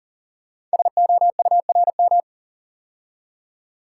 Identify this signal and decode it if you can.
Morse code, heard as a keyed tone, about 39 wpm, 690 Hz. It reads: SOURM
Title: Code Groups - 5 Letters - 40wpm